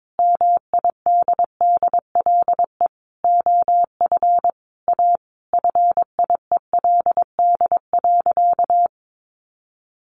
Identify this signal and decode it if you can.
Morse code, heard as a keyed tone, about 22 wpm, 700 Hz. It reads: MIDDLE OF A FIELD.
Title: Sentences from Top 1000 Words 22wpm